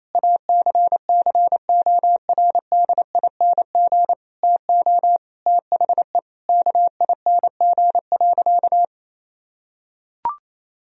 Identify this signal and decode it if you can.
Morse code, heard as a keyed tone, about 27 wpm, 700 Hz. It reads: ACCORDING TO THE KING.
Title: Sentences from Top 1000 Words - Review Part 4 - 28wpm